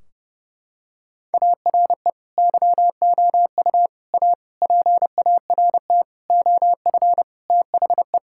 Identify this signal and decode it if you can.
Morse code, heard as a keyed tone, about 30 wpm, 700 Hz. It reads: ARE YOU A PART OF THE
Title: Sentences from Top 1600 Words 30wpm